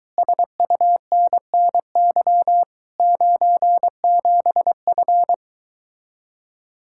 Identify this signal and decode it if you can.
Morse code, heard as a keyed tone, about 23 wpm, 700 Hz. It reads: SUNNY 97F